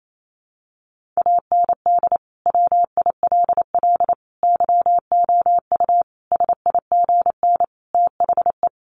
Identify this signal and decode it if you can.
Morse code, heard as a keyed tone, about 28 wpm, 700 Hz. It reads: AND WILL YOU SIGN THE
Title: Sentences from Top 1400 Words 28wpm